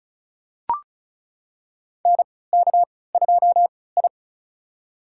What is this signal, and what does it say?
Morse code, heard as a keyed tone, about 36 wpm, spaced as for 20 wpm, 700 Hz. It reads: NK2I
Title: USA Call Signs - 4 Characters - 3x Character Spacing - 35wpm